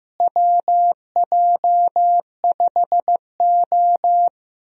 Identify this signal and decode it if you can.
Morse code, heard as a keyed tone, about 15 wpm, 700 Hz. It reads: WJ5O
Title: USA Call Signs - 4 Characters - Repeated 3x - 15wpm